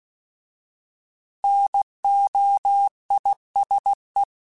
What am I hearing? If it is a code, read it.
Morse code, heard as a keyed tone, about 16 wpm, 800 Hz. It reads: NOISE
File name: Alphabetical / Alphabetical mp3s / Ns @15